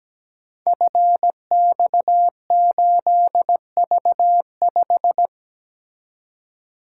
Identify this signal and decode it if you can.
Morse code, heard as a keyed tone, about 17 wpm, 700 Hz. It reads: FX8V5
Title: Code Groups - 5 Letter:Numbers - 17wpm